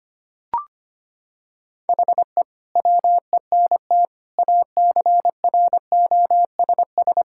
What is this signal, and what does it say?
Morse code, heard as a keyed tone, about 25 wpm, 700 Hz. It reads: HE WENT ACROSS
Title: Sentences from Top 1000 Words - Review Part 5 - 25wpm